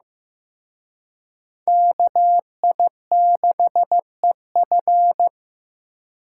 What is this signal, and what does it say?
Morse code, heard as a keyed tone, about 18 wpm, 700 Hz. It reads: KI6EF
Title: Speed Racing - USA 2x2 Call Signs 15wpm